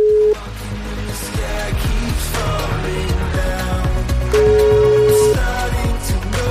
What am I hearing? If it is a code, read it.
Ringback tone (Europe, 425 Hz), with noise.